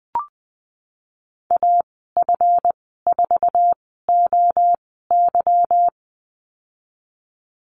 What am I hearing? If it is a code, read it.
Morse code, heard as a keyed tone, about 20 wpm, spaced as for 15 wpm, 700 Hz. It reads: AF4OY
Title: 186 Lesson 30g - Letter b - Call Signs 20wpm